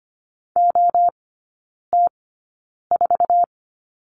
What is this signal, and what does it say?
Morse code, heard as a keyed tone, about 25 wpm, spaced as for 9 wpm, 690 Hz. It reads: OT4
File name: Lesson 14e - Number 2 - Three Characters 25wpm